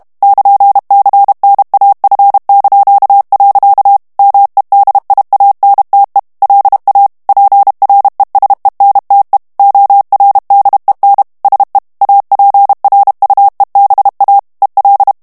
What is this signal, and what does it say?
Morse code, heard as a keyed tone, about 32 wpm, 800 Hz. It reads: <KN>CNAF). MEDIANTE LA PRESENTE ORDEN SE APRUEBA EL
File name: ESPAGNOL TEXTE EN CLAIR 2 VITESSE 30 MOTS MINUTES